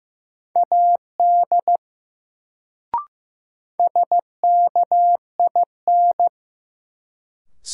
Morse code, heard as a keyed tone, about 15 wpm, 700 Hz. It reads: AD SKIN
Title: Sentences from Top 1300 Words - New Word Drill - 15wpm